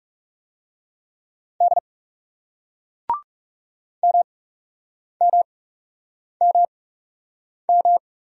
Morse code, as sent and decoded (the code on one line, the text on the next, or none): -.. / -- -- -- --
D MMMM